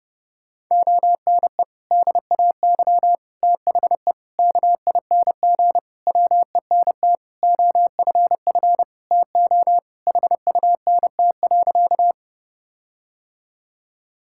Morse code, heard as a keyed tone, about 30 wpm, 700 Hz. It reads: ONE DAY THE KING WENT OFF TO HUNT.
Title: Sentences from Top 1600 Words 30wpm